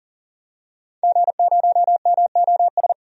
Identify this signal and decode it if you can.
Morse code, heard as a keyed tone, about 40 wpm, 700 Hz. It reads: G0MOS
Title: DXCC-100 Call Signs - 5 Characters - 40wpm